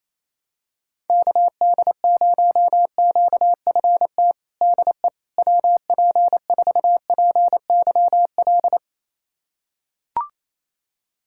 Morse code, heard as a keyed tone, about 27 wpm, 700 Hz. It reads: KD0QFT DE WP4PYL
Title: USA Callsign DE Callsign - 6 Characters - 28wpm